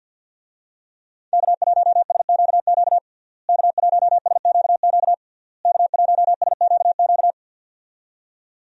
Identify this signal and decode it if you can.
Morse code, heard as a keyed tone, about 46 wpm, 700 Hz. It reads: K1SQQ K1SQQ K1SQQ
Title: USA Call Signs - 5 Characters - Repeated 3x - 50wpm